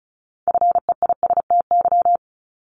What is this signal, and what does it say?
Morse code, heard as a keyed tone, about 36 wpm, 700 Hz. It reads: FEISTY
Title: Multi-letter Phonemes Repeated 3x 35wpm